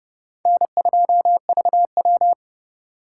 Morse code, heard as a keyed tone, about 29 wpm, 700 Hz. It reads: N2VW